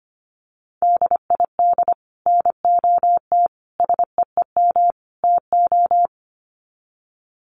Morse code, heard as a keyed tone, about 25 wpm, 700 Hz. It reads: DID NOT SEEM TO
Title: Sets of 4 Words 25wpm